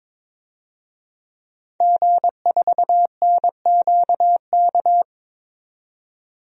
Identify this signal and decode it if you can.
Morse code, heard as a keyed tone, about 21 wpm, 700 Hz. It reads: G4NQK
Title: DXCC-100 Call Signs - 5 Characters - 22wpm